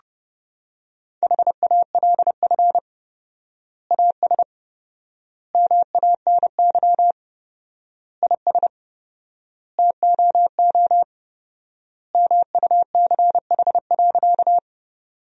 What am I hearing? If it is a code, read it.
Morse code, heard as a keyed tone, about 32 wpm, 700 Hz. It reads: HALF AS MANY IS TOO MUCH.
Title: 4x Word-Spacing - Sentences from Top 400 Words - 30wpm